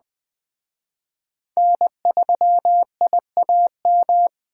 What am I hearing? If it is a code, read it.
Morse code, heard as a keyed tone, about 21 wpm, 700 Hz. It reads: N3IAM